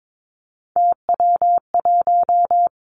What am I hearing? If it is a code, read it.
Morse code, heard as a keyed tone, about 22 wpm, 700 Hz. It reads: TW1
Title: Three Challenging Letter-Number Combinations 22wpm